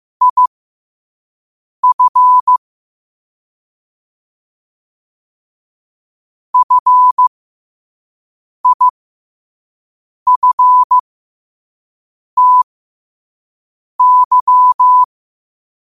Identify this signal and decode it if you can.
Morse code, heard as a keyed tone, about 15 wpm, spaced as for 5 wpm, 1000 Hz. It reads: IF FIFTY
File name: qotdfiles / 5wpm